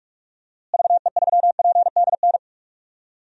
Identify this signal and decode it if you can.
Morse code, heard as a keyed tone, about 46 wpm, 700 Hz. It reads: VE3PDN
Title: DXCC-100 Call Signs - 6 Characters - Repeated 3x - 45wpm